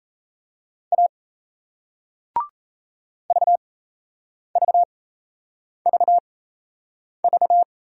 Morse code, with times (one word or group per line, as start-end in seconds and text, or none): 0.9-1.1 A
3.3-7.6 VVVV